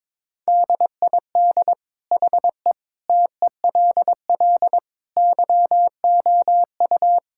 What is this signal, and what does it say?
Morse code, heard as a keyed tone, about 22 wpm, 700 Hz. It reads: DID HE TELL YOU
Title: Sentences from Top 1600 Words 22wpm